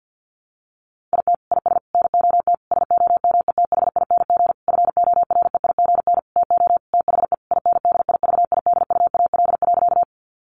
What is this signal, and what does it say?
Morse code, heard as a keyed tone, about 50 wpm, 700 Hz. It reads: IT IS NOT SOMETHING FOREIGN TO THE INDIVIDUAL.
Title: Sentences from Top 1400 Words 50wpm